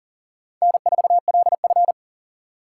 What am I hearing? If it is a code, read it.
Morse code, heard as a keyed tone, about 51 wpm, 700 Hz. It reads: N4LF